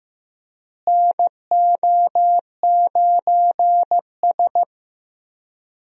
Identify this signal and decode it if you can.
Morse code, heard as a keyed tone, about 16 wpm, 700 Hz. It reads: NO9S
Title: USA Call Signs - 4 Characters - Repeated 3x - 15wpm